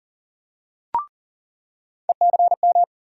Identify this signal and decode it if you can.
Morse code, heard as a keyed tone, about 40 wpm, 700 Hz. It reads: ECM
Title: USA Call Sign Postfixes - 3 Letters 40wpm